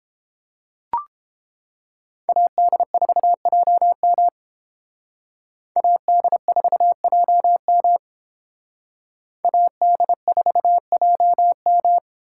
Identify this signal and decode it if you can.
Morse code, heard as a keyed tone, about 28 wpm, 700 Hz. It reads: AD4JM AD4JM AD4JM